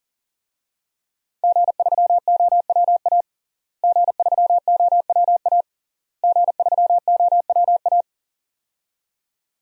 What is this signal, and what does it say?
Morse code, heard as a keyed tone, about 40 wpm, 700 Hz. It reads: G3OWA G3OWA G3OWA
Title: DXCC-100 Call Signs - 5 Characters - Repeated 3x - 40wpm